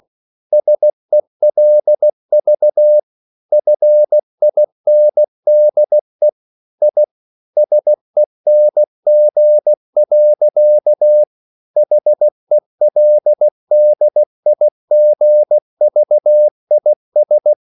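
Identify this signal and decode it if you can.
Morse code, heard as a keyed tone, about 16 wpm, 600 Hz. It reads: SELV FINDE I SENG. HELDIGVIS
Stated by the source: Slow (QRS) Morse code at 16 wpm, from CW med Gnister.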